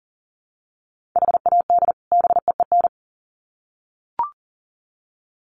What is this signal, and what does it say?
Morse code, heard as a keyed tone, about 40 wpm, 700 Hz. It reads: HAD BEEN
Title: Sets of 2 Words 40wpm